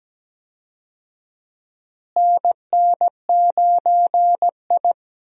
Morse code, heard as a keyed tone, about 17 wpm, 700 Hz. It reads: NN9I
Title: USA Call Signs - 4 Characters - Repeated 2x - 17wpm